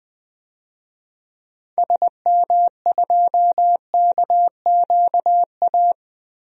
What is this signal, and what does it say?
Morse code, heard as a keyed tone, about 20 wpm, 700 Hz. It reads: SM2KQA
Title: DXCC-100 Call Signs - 6 Characters - Repeated 3x - 20wpm